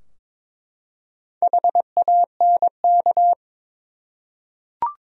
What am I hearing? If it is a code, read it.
Morse code, heard as a keyed tone, about 23 wpm, 700 Hz. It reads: HANK
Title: USA Common Names - Max 4 Letters - 22wpm